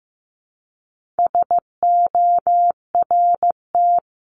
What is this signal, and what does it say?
Morse code, heard as a keyed tone, about 15 wpm, 700 Hz. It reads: SORT
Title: Top 1001-1250 Words 15wpm